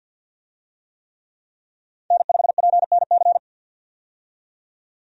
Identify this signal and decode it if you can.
Morse code, heard as a keyed tone, about 53 wpm, 700 Hz. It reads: N5PNC
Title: USA Call Signs - 5 Characters - 50wpm